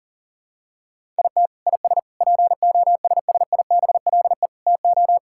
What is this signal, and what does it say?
Morse code, heard as a keyed tone, about 40 wpm, 700 Hz. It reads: IT IS POSSIBLE TO
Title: Sentences from Top 1000 Words - Review Part 6 - 40wpm